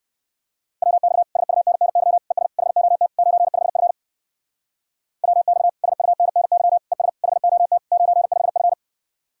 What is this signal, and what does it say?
Morse code, heard as a keyed tone, about 57 wpm, 700 Hz. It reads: WX SUNNY ES HOT 95F WX SUNNY ES HOT 95F